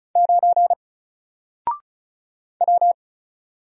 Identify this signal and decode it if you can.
Morse code, heard as a keyed tone, about 36 wpm, 700 Hz. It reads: 9 W
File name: Single-Letter-Number-35wpm